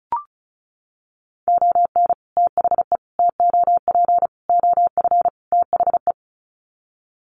Morse code, heard as a keyed tone, about 34 wpm, 700 Hz. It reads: ON THE TOP OF THE
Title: Sets of 5 Words Encore 35wpm